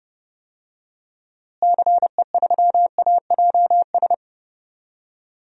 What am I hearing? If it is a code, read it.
Morse code, heard as a keyed tone, about 30 wpm, 700 Hz. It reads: CE3AJS
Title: DXCC-100 Call Signs - 6 Characters - Repeated 3x - 30wpm